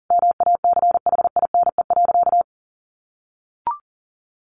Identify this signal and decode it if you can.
Morse code, heard as a keyed tone, about 37 wpm, 700 Hz. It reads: MACHINE.
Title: A Sound Of Thunder 40wpm